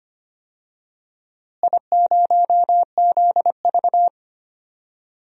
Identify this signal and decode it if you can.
Morse code, heard as a keyed tone, about 25 wpm, 700 Hz. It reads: I0ZV
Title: DXCC-100 Call Signs - 4 Characters - Repeated 2x - 25wpm